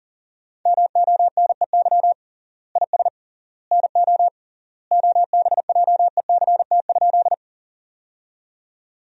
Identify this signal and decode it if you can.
Morse code, heard as a keyed tone, about 37 wpm, 700 Hz. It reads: MONEY IS NO OBJECT?
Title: 3x Word-Spacing - Sentences from Top 500 Words - 40wpm